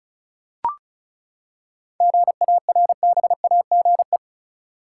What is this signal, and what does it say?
Morse code, heard as a keyed tone, about 36 wpm, 700 Hz. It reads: GARBAGE
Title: Seven Letter Words - 35wpm